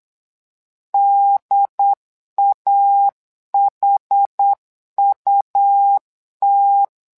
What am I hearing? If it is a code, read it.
Morse code, heard as a keyed tone, about 8 wpm, 800 Hz. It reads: DAHUT
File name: LEZIONI / 2 UHDN / 2 UHD 8wpm 2258